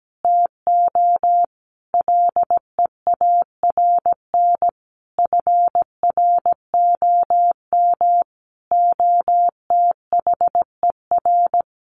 Morse code, heard as a keyed tone, about 17 wpm, 700 Hz. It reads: TO LEARN FROM OTHER
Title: Sentences from Top 1100 Words 17wpm